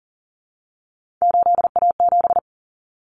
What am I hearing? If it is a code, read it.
Morse code, heard as a keyed tone, about 40 wpm, 700 Hz. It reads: 8A7
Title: Common Cut Numbers - 3-Digit 40wpm